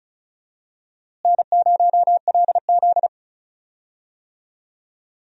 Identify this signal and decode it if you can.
Morse code, heard as a keyed tone, about 36 wpm, 700 Hz. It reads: N0LZ
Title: USA Call Signs - 4 Characters - Repeated 2x - 35wpm